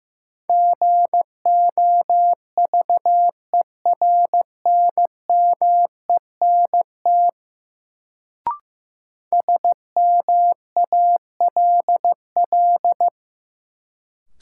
Morse code, heard as a keyed tone, about 15 wpm, 700 Hz. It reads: GOVERNMENT SMALL
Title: Sets of 2 Words - New Word Drill - 15wpm